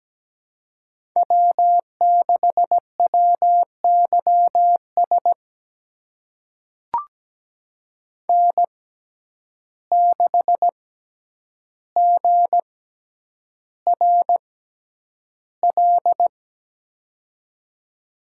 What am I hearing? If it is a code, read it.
Morse code, heard as a keyed tone, about 17 wpm, spaced as for 9 wpm, 700 Hz. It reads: W6WYS N6GRL
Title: USA Call Signs - 5 Characters - 6x Character Spacing - 17wpm